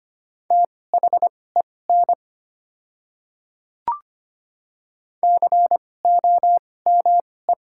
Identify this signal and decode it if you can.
Morse code, heard as a keyed tone, about 25 wpm, spaced as for 18 wpm, 700 Hz. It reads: THEN COME
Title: 2x Character Spacing - Top 100 Words - 25wpm